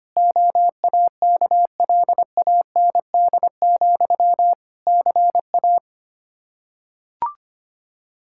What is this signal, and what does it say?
Morse code, heard as a keyed tone, about 25 wpm, 700 Hz. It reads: OAKLAND, CA
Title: K1USN SST QSO Practice 25wpm